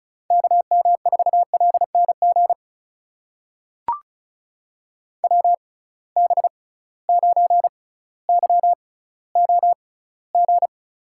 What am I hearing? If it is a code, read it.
Morse code, heard as a keyed tone, about 34 wpm, spaced as for 18 wpm, 700 Hz. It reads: KM4LNG WB9YOG
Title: USA Call Signs - 6 Characters - 6x Character Spacing - 35wpm